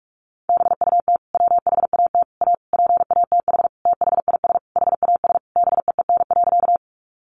Morse code, heard as a keyed tone, about 44 wpm, 700 Hz. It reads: BUT WHAT A PATH THIS HAS BEEN.
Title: Sentences from Top 1400 Words 45wpm